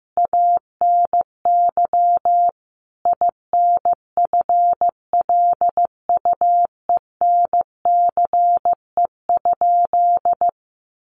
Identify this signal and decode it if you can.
Morse code, heard as a keyed tone, about 15 wpm, 700 Hz. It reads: ANY INFLUENCE?
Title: Sentences from Top 1100 Words 15wpm